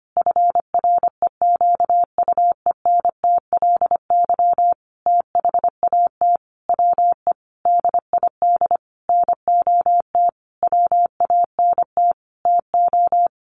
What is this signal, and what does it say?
Morse code, heard as a keyed tone, about 25 wpm, 700 Hz. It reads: FREQUENTLY THAT WE DID NOT WANT TO
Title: Sentences from Top 1300 Words 25wpm